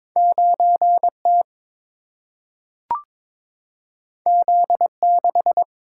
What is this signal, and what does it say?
Morse code, heard as a keyed tone, about 22 wpm, 700 Hz. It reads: 9T Z6
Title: Single-Letter Single-Number 22wpm